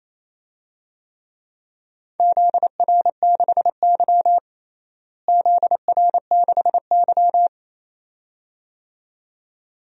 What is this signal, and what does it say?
Morse code, heard as a keyed tone, about 28 wpm, 700 Hz. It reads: ZR6Y ZR6Y
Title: DXCC-100 Call Signs - 4 Characters - Repeated 2x - 28wpm